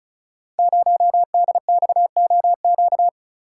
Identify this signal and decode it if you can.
Morse code, heard as a keyed tone, about 36 wpm, 700 Hz. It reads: 0DXOQ
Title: Code Groups - 5 Letter:Numbers - 35wpm